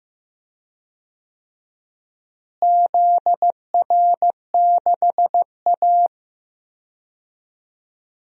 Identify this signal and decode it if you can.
Morse code, heard as a keyed tone, about 15 wpm, 700 Hz. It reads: ZR6A
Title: DXCC-100 Call Signs - 4 Characters - Repeated 2x - 15wpm